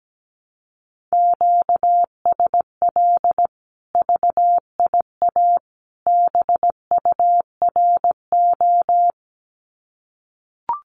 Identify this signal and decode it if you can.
Morse code, heard as a keyed tone, about 17 wpm, 700 Hz. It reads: QSL VIA BURO
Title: Common QSO Phrases 17wpm